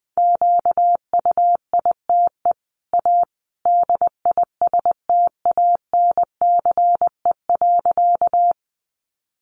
Morse code, heard as a keyed tone, about 20 wpm, 700 Hz. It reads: QUITE A DISTANCE.
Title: Sentences from Top 1200 Words 20wpm